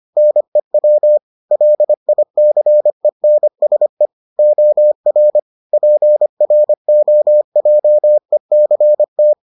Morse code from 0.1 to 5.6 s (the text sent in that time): NEW LICENSE OR